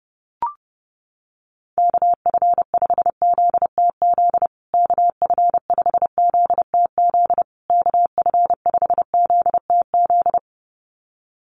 Morse code, heard as a keyed tone, about 30 wpm, 700 Hz. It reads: KF5ZTZ KF5ZTZ KF5ZTZ
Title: USA Call Signs Repeated 3x 30wpm